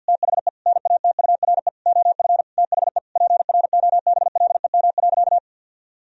Morse code, heard as a keyed tone, about 50 wpm, 700 Hz. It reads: THE NATURE OF THE PROBLEM.
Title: Sentences from Top 1000 Words - Review Part 2 - 50wpm